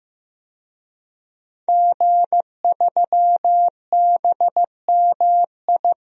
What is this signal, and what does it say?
Morse code, heard as a keyed tone, about 15 wpm, 700 Hz. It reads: G3BMI
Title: DXCC-100 Call Signs - 5 Characters - Repeated 2x - 15wpm